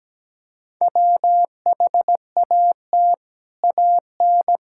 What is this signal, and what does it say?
Morse code, heard as a keyed tone, about 17 wpm, 700 Hz. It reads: WHAT AN
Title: Sentences from Top 1600 Words 17wpm